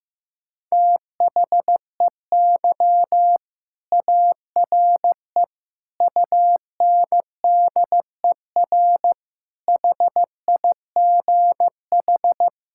Morse code, heard as a keyed tone, about 15 wpm, 700 Hz. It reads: THEY ARE UNDER HIGH
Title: Sentences from Top 1600 Words 15wpm